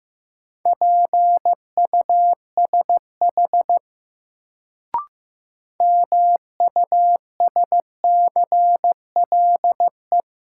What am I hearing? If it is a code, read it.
Morse code, heard as a keyed tone, about 15 wpm, 700 Hz. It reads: PUSH MUSCLE
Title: Sentences from Top 1300 Words - New Word Drill - 15wpm